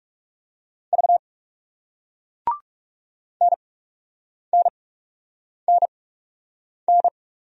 Morse code, heard as a keyed tone, about 36 wpm, spaced as for 8 wpm, 700 Hz. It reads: V NNNN
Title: Speed-Racing Single Letters 30wpm